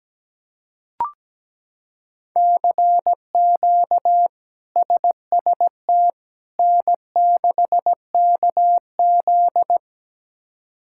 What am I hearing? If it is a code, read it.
Morse code, heard as a keyed tone, about 17 wpm, 700 Hz. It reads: CQ SST N6KZ
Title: K1USN SST QSO Practice - 17wpm